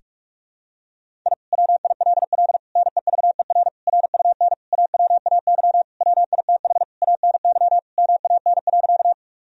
Morse code, heard as a keyed tone, about 46 wpm, 700 Hz. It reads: I WILL NEVER RUN AWAY WITH ANY MAN.
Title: Sentences from Top 1000 Words - Review Part 1 - 45wpm